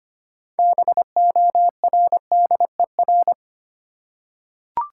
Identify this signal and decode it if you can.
Morse code, heard as a keyed tone, about 25 wpm, 700 Hz. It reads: BORDER
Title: Six Letter Words - 25wpm